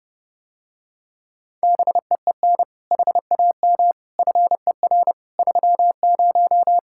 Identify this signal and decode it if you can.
Morse code, heard as a keyed tone, about 30 wpm, 700 Hz. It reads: BEEN HAM FER 30